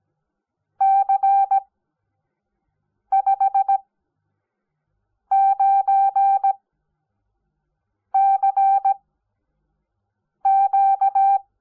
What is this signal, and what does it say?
Morse code, heard as a keyed tone, about 17 wpm, spaced as for 5 wpm, 780 Hz. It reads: C59CQ
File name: Onlinelehrgang / morsen / DJ4UF Morsen05